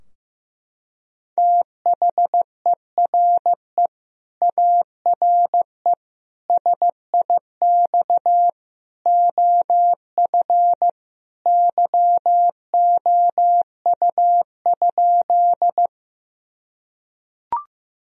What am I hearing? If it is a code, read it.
Morse code, heard as a keyed tone, about 16 wpm, 700 Hz. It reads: THERE ARE SIX OF YOU?
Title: Sentences from Top 1000 Words - Review Part 2 - 15wpm